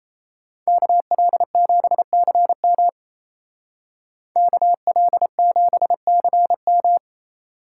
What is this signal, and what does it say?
Morse code, heard as a keyed tone, about 33 wpm, 700 Hz. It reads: KL7CM KL7CM
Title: Speed Racing - USA 2x2 Call Signs 28wpm